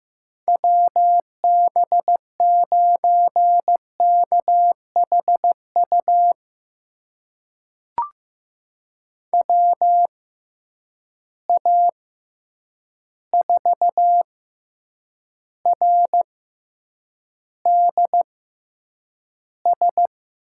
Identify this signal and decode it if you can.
Morse code, heard as a keyed tone, about 15 wpm, spaced as for 8 wpm, 700 Hz. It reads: WB9KHU WA4RDS
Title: USA Call Signs - 6 Characters - 6x Character Spacing - 15wpm